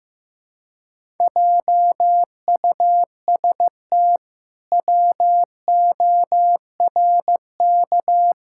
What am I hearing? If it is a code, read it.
Morse code, heard as a keyed tone, about 16 wpm, 700 Hz. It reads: JUST WORK
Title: Pangram Challenge v2 - 15wpm